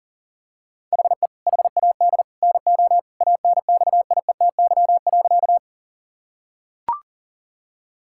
Morse code, heard as a keyed tone, about 40 wpm, 700 Hz. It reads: HE HAD NO ANXIETY.
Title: Sentences from Top 1600 Words 40wpm